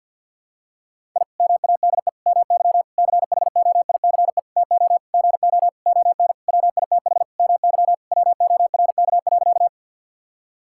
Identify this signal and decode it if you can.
Morse code, heard as a keyed tone, about 52 wpm, 700 Hz. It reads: I MADE MY CHOICE TO GO ON WITH MY WORK.
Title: Sentences from Top 1000 Words - Review Part 6 - 50wpm